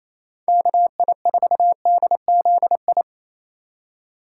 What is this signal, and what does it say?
Morse code, heard as a keyed tone, about 28 wpm, 700 Hz. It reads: KI4DZI
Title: USA Call Signs - 6 Characters - 3x Character Spacing - 28wpm